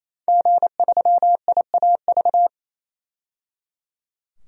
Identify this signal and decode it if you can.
Morse code, heard as a keyed tone, about 28 wpm, 700 Hz. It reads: G3IAV
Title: DXCC-100 Call Signs - 5 Characters - 28wpm